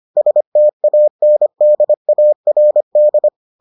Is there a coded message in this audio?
Yes: Morse code from 0.2 to 3.3 s, heard as a keyed tone, about 25 wpm, 600 Hz. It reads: STANDARD